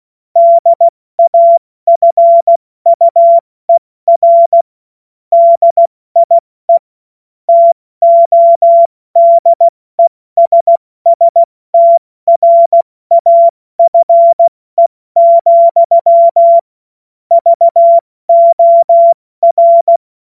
Morse code, heard as a keyed tone, about 16 wpm, 680 Hz. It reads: DAFUER DIE TODESSTRAFE, VOR
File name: Morsekurs / Witze / Witz 19. April 2022 70 F80